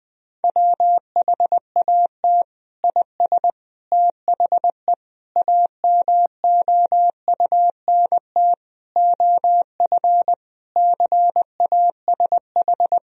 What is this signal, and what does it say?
Morse code, heard as a keyed tone, about 20 wpm, 700 Hz. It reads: WHAT IS THE AMOUNT OF CASH
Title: Sentences from Top 1000 Words - Review Part 4 - 20wpm